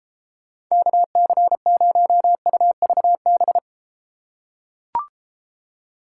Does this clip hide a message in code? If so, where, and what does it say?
Yes: Morse code from 0.7 to 3.6 s, heard as a keyed tone, about 33 wpm, 700 Hz. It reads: KC0UVB